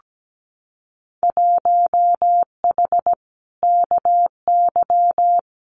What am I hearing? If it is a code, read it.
Morse code, heard as a keyed tone, about 17 wpm, 700 Hz. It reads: 1H KY
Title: Winter Field Day Exchanges 17wpm